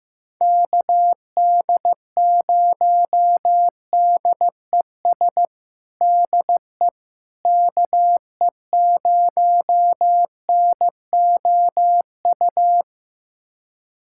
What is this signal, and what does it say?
Morse code, heard as a keyed tone, about 15 wpm, 700 Hz. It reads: KD0DES DE KE0NOU
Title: USA Callsign DE Callsign - 6 Characters - 15wpm